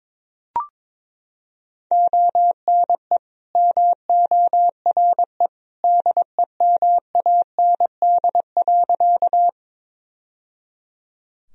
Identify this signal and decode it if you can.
Morse code, heard as a keyed tone, about 21 wpm, 700 Hz. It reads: ONE MORE DEMAND.
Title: Sentences from Top 1000 Words - Review Part 4 - 22wpm